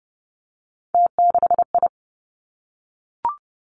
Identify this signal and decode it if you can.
Morse code, heard as a keyed tone, about 31 wpm, 700 Hz. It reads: T6I
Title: Three Challenging Letter-Number Combinations 30wpm